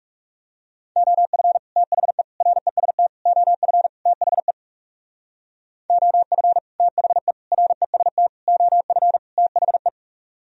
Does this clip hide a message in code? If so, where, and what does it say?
Yes: Morse code from 1.0 to 9.9 s, heard as a keyed tone, about 39 wpm, 700 Hz. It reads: OF THE REST OF THE OF THE REST OF THE